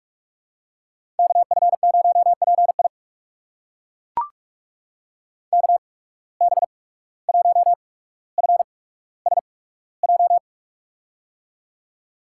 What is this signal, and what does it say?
Morse code, heard as a keyed tone, about 42 wpm, spaced as for 18 wpm, 700 Hz. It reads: KF0PI KB1FSJ
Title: USA Call Signs - 8x Character Spacing - 45wpm